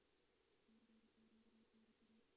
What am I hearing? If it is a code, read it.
Morse code, heard as a keyed tone, about 31 wpm, 220 Hz, with noise. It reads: CQ NA
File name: 2016 JAN CW NAQP / 2016 Jan naqp k5ot - 0-3